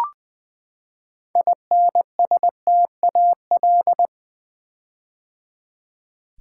Morse code, heard as a keyed tone, about 20 wpm, 700 Hz. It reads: INSTAL
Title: Build A Word - 6 Letters At A Time - 12-Letter Words - 20wpm